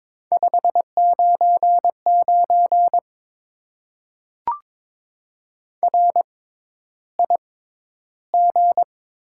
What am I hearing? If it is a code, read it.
Morse code, heard as a keyed tone, about 22 wpm, spaced as for 11 wpm, 700 Hz. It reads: 599 RIG
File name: Top 90 QSO Elements - 6x Character Spacing - 22wpm